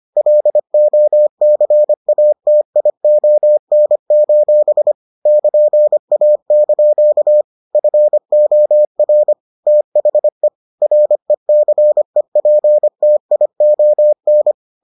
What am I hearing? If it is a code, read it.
Morse code, heard as a keyed tone, about 25 wpm, 600 Hz. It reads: LOCATION: <KN>A) FOR THE RECEPTION